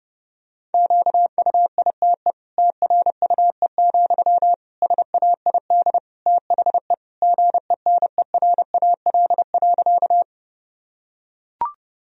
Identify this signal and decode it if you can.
Morse code, heard as a keyed tone, about 30 wpm, 700 Hz. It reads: QUITE TRUE, SAID THE GENERAL.
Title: Sentences from Top 1000 Words - Review Part 3 - 30wpm